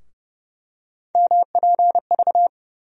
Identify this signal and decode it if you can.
Morse code, heard as a keyed tone, about 30 wpm, 700 Hz. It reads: MPV
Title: USA Call Sign Postfixes - 3 Letters 30wpm